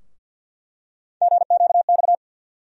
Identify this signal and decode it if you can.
Morse code, heard as a keyed tone, about 53 wpm, 700 Hz. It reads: GQX
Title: USA Call Sign Postfixes - 3 Letters 50wpm